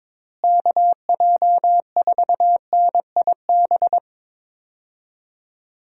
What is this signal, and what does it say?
Morse code, heard as a keyed tone, about 22 wpm, 700 Hz. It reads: KJ4NIB
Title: USA Call Signs - 6 Characters - Repeated 3x - 22wpm